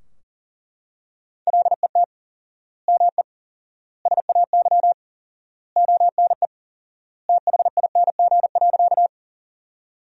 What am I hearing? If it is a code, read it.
Morse code, heard as a keyed tone, about 40 wpm, 700 Hz. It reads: LET ME SAY ONE THING.
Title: 4x Word-Spacing - Sentences from Top 300 Words - 40wpm